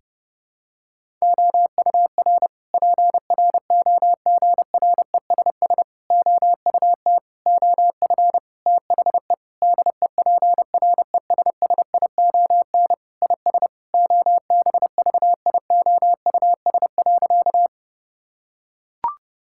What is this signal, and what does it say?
Morse code, heard as a keyed tone, about 30 wpm, 700 Hz. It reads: OUR PROGRESS OUT OF THE DEPRESSION IS OBVIOUS.
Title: Sentences from Top 1600 Words 30wpm